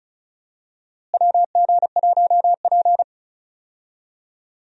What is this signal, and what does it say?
Morse code, heard as a keyed tone, about 36 wpm, 700 Hz. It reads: WG1P
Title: USA Call Signs - 4 Characters - Repeated 2x - 35wpm